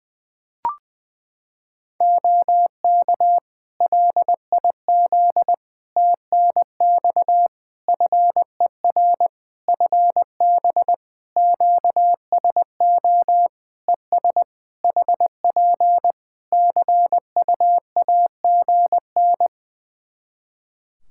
Morse code, heard as a keyed tone, about 20 wpm, 700 Hz. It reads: OK LIZ TNX FER FB QSO ES HP CUAGN
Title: K1USN SST QSO Practice 20wpm